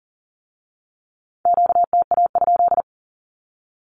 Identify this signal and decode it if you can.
Morse code, heard as a keyed tone, about 39 wpm, 690 Hz. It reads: QTA?
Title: Q-Codes with Spoken Meaning - 40wpm